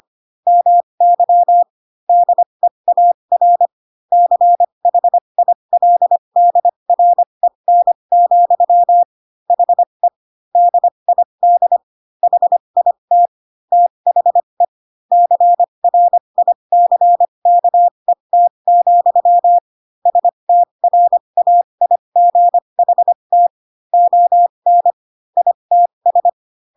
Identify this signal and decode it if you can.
Morse code, heard as a keyed tone, about 25 wpm, spaced as for 22 wpm, 700 Hz. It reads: MY DEAR CHILDREN, HE DID HIT THE CRICKET, STRAIGHT ON ITS
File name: Pinocchio / Pinocchio ch4 18 25 ik1hge